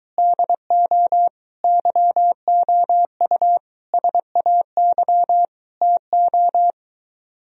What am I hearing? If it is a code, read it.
Morse code, heard as a keyed tone, about 23 wpm, 700 Hz. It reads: DO YOU SAY TO